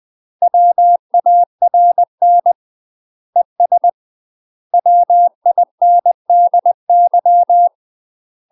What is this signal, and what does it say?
Morse code, heard as a keyed tone, about 20 wpm, 700 Hz. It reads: WARN ES WINDY
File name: UK MASTER QSOs TRAINING / 20WPM / UK MASTER QSOs 0017